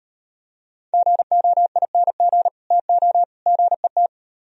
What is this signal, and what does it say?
Morse code, heard as a keyed tone, about 31 wpm, 700 Hz. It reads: GOING TO GET
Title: Speed-Racing - Sets of 3 Words 25wpm